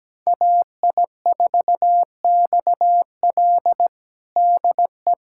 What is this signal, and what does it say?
Morse code, heard as a keyed tone, about 17 wpm, 700 Hz. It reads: AI4XL DE
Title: USA Callsign DE Callsign - 5 Characters - 17wpm